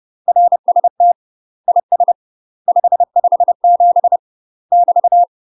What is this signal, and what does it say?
Morse code, heard as a keyed tone, about 30 wpm, 700 Hz. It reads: RST IS 557 <BT>
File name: UK MASTER QSOs TRAINING / 30WPM / UK MASTER QSOs 0044